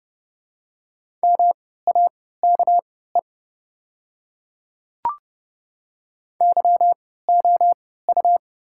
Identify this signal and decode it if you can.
Morse code, heard as a keyed tone, about 30 wpm, spaced as for 17 wpm, 700 Hz. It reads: MAKE YOU
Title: 3x Character Spacing - Top 100 Words - 30wpm